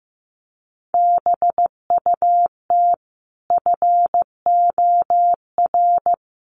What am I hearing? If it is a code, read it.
Morse code, heard as a keyed tone, about 15 wpm, 700 Hz. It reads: BUT FOR
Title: Sets of 5 Words Encore 15wpm